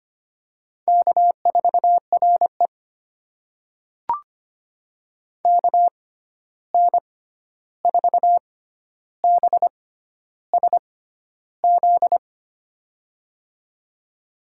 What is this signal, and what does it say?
Morse code, heard as a keyed tone, about 25 wpm, spaced as for 11 wpm, 700 Hz. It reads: K4RE KN4BSZ
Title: USA Call Signs - 6x Character Spacing - 25wpm